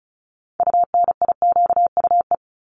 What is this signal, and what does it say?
Morse code, heard as a keyed tone, about 36 wpm, 700 Hz. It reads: UNIQUE
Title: Multi-letter Phonemes 35wpm